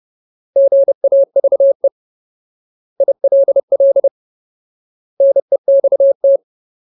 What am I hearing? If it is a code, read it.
Morse code, heard as a keyed tone, about 30 wpm, 550 Hz. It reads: GAVE ILL NEXT
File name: Ultimate Word List / 30WPM Ultimate